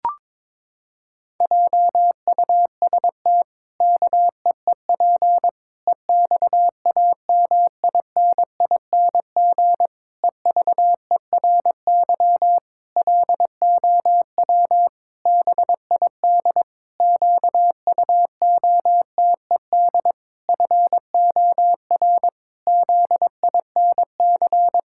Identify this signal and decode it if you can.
Morse code, heard as a keyed tone, about 22 wpm, 700 Hz. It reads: JUST KEEP EXAMINING EVERY LOW BID QUOTED FOR ZINC
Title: Pangram Challenge v2 - 22wpm